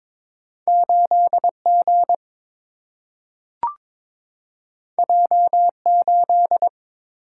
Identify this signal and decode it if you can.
Morse code, heard as a keyed tone, about 22 wpm, 700 Hz. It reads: 8 G J 8
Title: Single-Letter Single-Number 22wpm